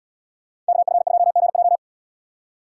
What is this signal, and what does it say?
Morse code, heard as a keyed tone, about 75 wpm, 700 Hz. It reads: KF2GJ